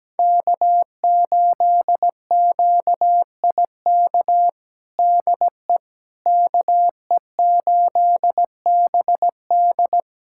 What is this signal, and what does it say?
Morse code, heard as a keyed tone, about 17 wpm, 700 Hz. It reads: K8QIK DE KE8BD
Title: USA Callsign DE Callsign - 5 Characters - 17wpm